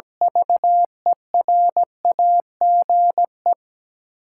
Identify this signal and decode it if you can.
Morse code, heard as a keyed tone, about 17 wpm, 700 Hz. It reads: VERAGE
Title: Sets of 5 Words - New Word Drill - 17wpm